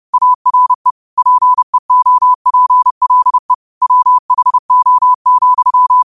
Morse code, heard as a keyed tone, about 30 wpm, 1000 Hz. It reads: ARE PEOPLE WHO,
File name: qotdfiles / 30wpm